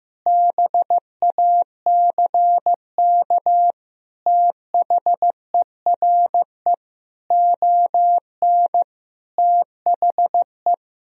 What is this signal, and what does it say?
Morse code, heard as a keyed tone, about 15 wpm, 700 Hz. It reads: BACK THERE ON THE
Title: Sentences from Top 1000 Words - Review Part 2 - 15wpm